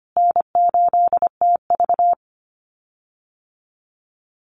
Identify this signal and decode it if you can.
Morse code, heard as a keyed tone, about 25 wpm, 700 Hz. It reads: N8TV
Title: USA Call Signs Repeated 3x 25wpm